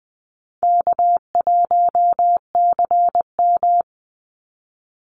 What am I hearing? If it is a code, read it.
Morse code, heard as a keyed tone, about 20 wpm, 700 Hz. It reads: K1CM
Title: NA SOTA Chaser Callsigns 20wpm